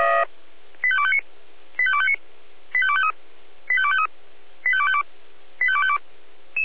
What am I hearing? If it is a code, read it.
Five-tone selective call (ZVEI), heard as quick runs of five tones, tones of about 70 ms. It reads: 85259 85259 85253 85253 85252 85252
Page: SDR/ZVEI